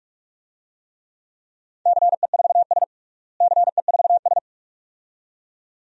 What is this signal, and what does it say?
Morse code, heard as a keyed tone, about 43 wpm, 700 Hz. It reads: CE4S CE4S
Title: DXCC-100 Call Signs - 4 Characters - Repeated 2x - 45wpm